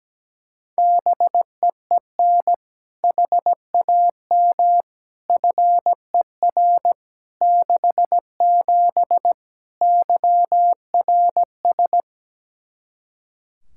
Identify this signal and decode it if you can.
Morse code, heard as a keyed tone, about 17 wpm, 700 Hz. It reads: BEEN HAM FER 67 YRS
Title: Practice QSO - Exchange 3 - More Info - 17wpm